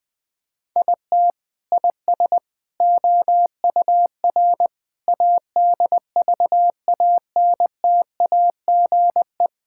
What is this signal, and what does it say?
Morse code, heard as a keyed tone, about 20 wpm, 700 Hz. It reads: IT IS OUR ADVANTAGE
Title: Sentences from Top 1000 Words - Review Part 6 - 20wpm